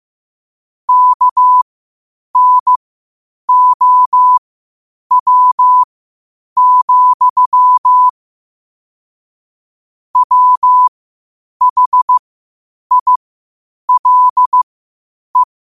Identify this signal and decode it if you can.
Morse code, heard as a keyed tone, about 15 wpm, spaced as for 8 wpm, 1000 Hz. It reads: KNOW, WHILE
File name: qotdfiles / 8wpm